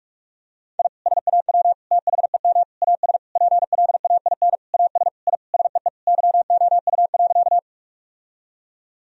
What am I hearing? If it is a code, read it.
Morse code, heard as a keyed tone, about 41 wpm, 700 Hz. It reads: I SAW THEM AS PLAIN AS I SEE YOU.
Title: Sentences from Top 1000 Words - Review Part 2 - 45wpm